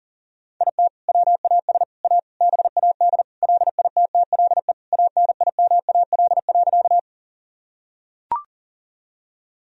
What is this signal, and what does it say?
Morse code, heard as a keyed tone, about 40 wpm, 700 Hz. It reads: IT WAS A BAD LITTLE ANIMAL.
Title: Sentences from Top 1000 Words - Review Part 1 - 40wpm